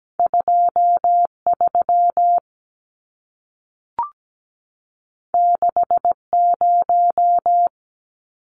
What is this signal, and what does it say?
Morse code, heard as a keyed tone, about 17 wpm, 700 Hz. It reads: 23 60
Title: Numbers - 2-Digit 17wpm